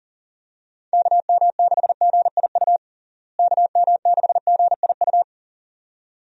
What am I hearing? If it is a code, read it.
Morse code, heard as a keyed tone, about 40 wpm, 700 Hz. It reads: KM6GIU KM6GIU
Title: USA Call Signs - 6 Characters - Repeated 2x - 40wpm